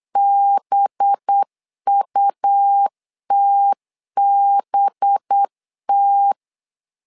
Morse code, heard as a keyed tone, about 8 wpm, 800 Hz. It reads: BUTBT
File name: LEZIONI / 5 LBVC / 5 LB 8wpm 2816